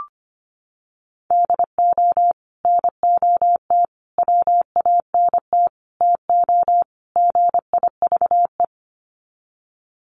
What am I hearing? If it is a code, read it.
Morse code, heard as a keyed tone, about 25 wpm, 700 Hz. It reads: DO NOT WANT TO GIVE
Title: Sets of 5 Words Encore 25wpm